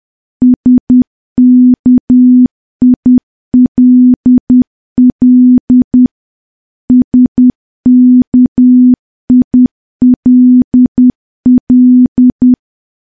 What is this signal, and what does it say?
Morse code, heard as a keyed tone, about 10 wpm, 250 Hz. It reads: SKILL SKILL
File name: morse / morse-training-10WPM